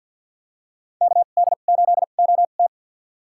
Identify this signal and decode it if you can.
Morse code, heard as a keyed tone, about 49 wpm, spaced as for 35 wpm, 700 Hz. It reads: KD8OT
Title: USA Call Signs - 2x Character Spacing - 50wpm